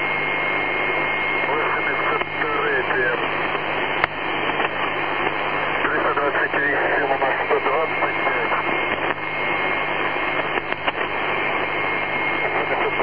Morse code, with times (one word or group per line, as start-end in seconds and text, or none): none